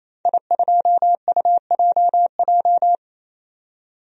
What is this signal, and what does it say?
Morse code, heard as a keyed tone, about 28 wpm, 700 Hz. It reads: I2UJJ
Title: DXCC-100 Call Signs - 5 Characters - 28wpm